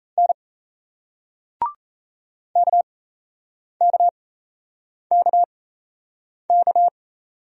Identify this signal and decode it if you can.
Morse code, heard as a keyed tone, about 38 wpm, spaced as for 8 wpm, 700 Hz. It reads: N KKKK